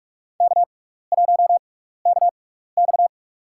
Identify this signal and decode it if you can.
Morse code, heard as a keyed tone, about 46 wpm, spaced as for 16 wpm, 700 Hz. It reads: K1KX
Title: USA Call Signs - 6x Character Spacing - 45wpm